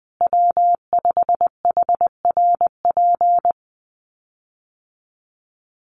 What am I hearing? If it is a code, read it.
Morse code, heard as a keyed tone, about 20 wpm, 700 Hz. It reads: W5HRP
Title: USA Call Signs Repeated 2x 20wpm